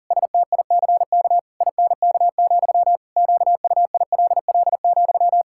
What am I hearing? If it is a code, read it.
Morse code, heard as a keyed tone, about 40 wpm, 700 Hz. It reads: STICK INK, QUILL,
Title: Pangram Challenge v2 - 40wpm